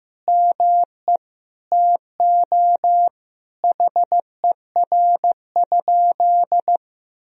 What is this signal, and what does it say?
Morse code, heard as a keyed tone, about 15 wpm, 700 Hz. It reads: ME TO HER?
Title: Sentences from Top 200 Words and Abbreviations 15wpm